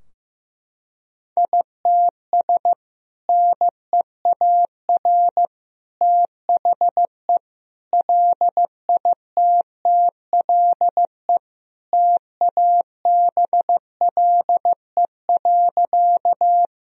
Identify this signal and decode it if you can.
Morse code, heard as a keyed tone, about 15 wpm, 700 Hz. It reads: ITS NEAR THE LITTLE TABLE.
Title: Sentences from Top 1000 Words - Review Part 2 - 15wpm